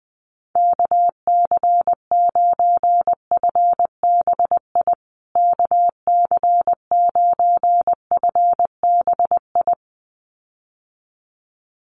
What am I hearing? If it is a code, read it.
Morse code, heard as a keyed tone, about 20 wpm, 700 Hz. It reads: KC9FBI KC9FBI
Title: USA Call Signs Repeated 2x 20wpm